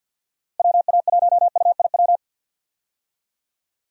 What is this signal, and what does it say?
Morse code, heard as a keyed tone, about 53 wpm, 700 Hz. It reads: WA1UIW
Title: USA Call Signs - 6 Characters - Repeated 3x - 50wpm